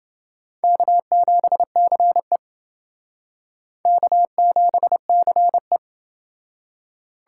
Morse code, heard as a keyed tone, about 28 wpm, 700 Hz. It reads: K7CE K7CE